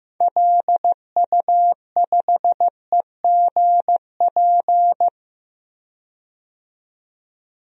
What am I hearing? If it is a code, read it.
Morse code, heard as a keyed tone, about 15 wpm, 700 Hz. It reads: LU5EGP
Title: DXCC-100 Call Signs - 6 Characters - 15wpm